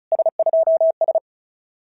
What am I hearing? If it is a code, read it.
Morse code, heard as a keyed tone, about 36 wpm, 650 Hz. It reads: S2S
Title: SOTA QSO Practice 35wpm